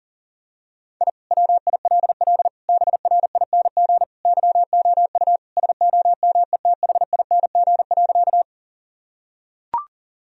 Morse code, heard as a keyed tone, about 40 wpm, 700 Hz. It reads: I WILL BRING YOU SOMETHING.
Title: Sentences from Top 1000 Words - Review Part 3 - 40wpm